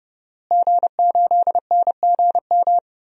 Morse code, heard as a keyed tone, about 30 wpm, 700 Hz. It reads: G8NGM
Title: DXCC-100 Call Signs - 5 Characters - Repeated 3x - 30wpm